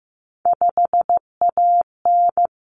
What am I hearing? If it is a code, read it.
Morse code, heard as a keyed tone, about 15 wpm, 700 Hz. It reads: 5AN
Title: Common Cut Numbers - 3-Digit 15wpm